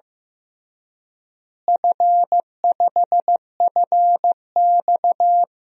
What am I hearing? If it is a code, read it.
Morse code, heard as a keyed tone, about 16 wpm, 700 Hz. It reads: F5FX
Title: DXCC-100 Call Signs - 4 Characters - Repeated 3x - 15wpm